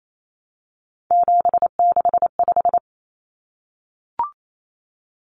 Morse code, heard as a keyed tone, about 28 wpm, 700 Hz. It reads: 765
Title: Numbers - 3-Digit 28wpm